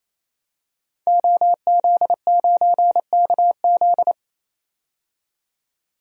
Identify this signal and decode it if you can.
Morse code, heard as a keyed tone, about 28 wpm, 700 Hz. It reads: OZ9KZ
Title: DXCC-100 Call Signs - 5 Characters - 28wpm